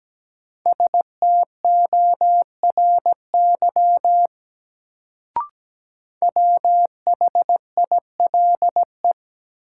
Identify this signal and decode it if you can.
Morse code, heard as a keyed tone, about 17 wpm, 700 Hz. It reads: STORY WHILE
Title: Sentences from Top 300 Words - New Word Drill - 17wpm